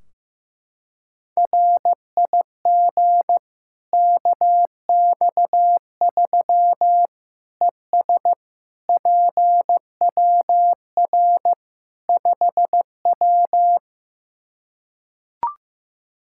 Morse code, heard as a keyed tone, about 15 wpm, 700 Hz. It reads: RIG KX3 ES PWR 5W
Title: Practice QSO - Exchange 2 - Info - 15wpm